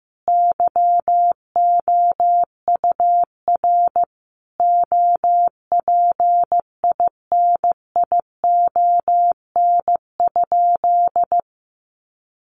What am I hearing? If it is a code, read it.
Morse code, heard as a keyed tone, about 15 wpm, 700 Hz. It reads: YOUR OPINION?
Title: Sentences from Top 1200 Words 15wpm